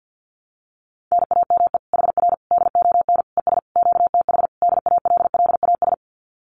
Morse code, heard as a keyed tone, about 50 wpm, 700 Hz. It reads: NAME HR DON ES QTH DALLAS
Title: Common QSO Phrases Encore 50wpm